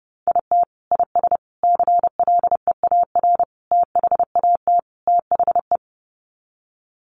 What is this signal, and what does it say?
Morse code, heard as a keyed tone, about 30 wpm, 700 Hz. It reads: IT IS CLEAR THAT THE
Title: Sets of 5 Words Encore 30wpm